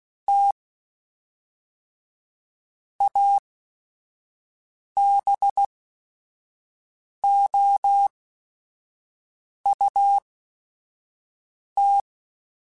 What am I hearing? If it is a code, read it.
Morse code, heard as a keyed tone, about 16 wpm, spaced as for 5 wpm, 800 Hz. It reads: T ABOUT